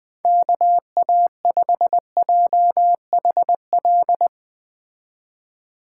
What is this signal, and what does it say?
Morse code, heard as a keyed tone, about 20 wpm, 700 Hz. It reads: KA5JHL